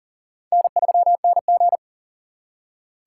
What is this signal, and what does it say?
Morse code, heard as a keyed tone, about 51 wpm, 700 Hz. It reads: N3NG